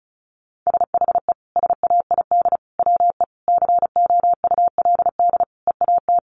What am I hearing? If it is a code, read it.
Morse code, heard as a keyed tone, about 34 wpm, 700 Hz. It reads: SHE SAID WE COULD EAT
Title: Sentences from Top 1300 Words 35wpm